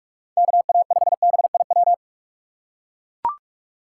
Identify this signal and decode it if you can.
Morse code, heard as a keyed tone, about 43 wpm, 700 Hz. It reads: KA5BIW
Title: USA Call Signs - 6 Characters - 4x Character Spacing - 45wpm